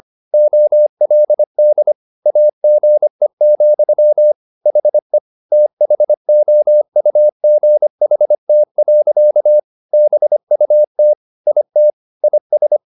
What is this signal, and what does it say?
Morse code, heard as a keyed tone, about 25 wpm, 600 Hz. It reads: OLD AGE, HE THOUGHT. BUT IT IS
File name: the old man and the sea part 1 25wpm